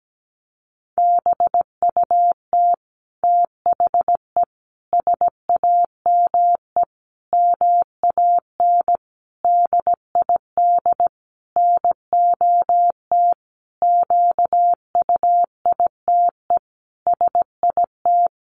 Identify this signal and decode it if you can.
Morse code, heard as a keyed tone, about 17 wpm, 700 Hz. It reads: BUT THE SAME MAN DID NOT QUITE SIT
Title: A Sound Of Thunder 17wpm